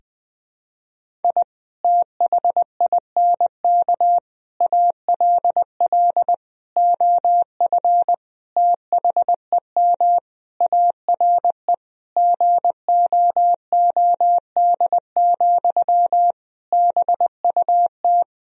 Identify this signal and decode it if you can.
Morse code, heard as a keyed tone, about 20 wpm, 700 Hz. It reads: I THINK ALL OF THEM ARE GOOD, BUT
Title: Sentences from Top 1000 Words - Review Part 4 - 20wpm